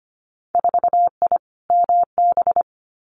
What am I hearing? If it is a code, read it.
Morse code, heard as a keyed tone, about 25 wpm, 690 Hz. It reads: 4I MB
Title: Winter Field Day Exchanges Encore 25wpm